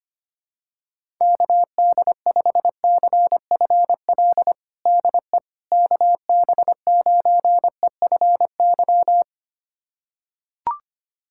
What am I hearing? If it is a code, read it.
Morse code, heard as a keyed tone, about 25 wpm, 700 Hz. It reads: KD5CFL DE KB9EFY
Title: USA Callsign DE Callsign 25wpm